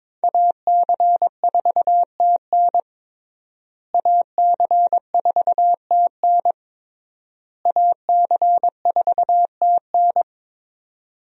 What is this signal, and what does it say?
Morse code, heard as a keyed tone, about 22 wpm, 700 Hz. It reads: AC4TN AC4TN AC4TN
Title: USA Call Signs - 5 Characters - Repeated 3x - 22wpm